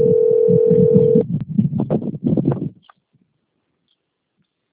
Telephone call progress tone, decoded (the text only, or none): ringback tone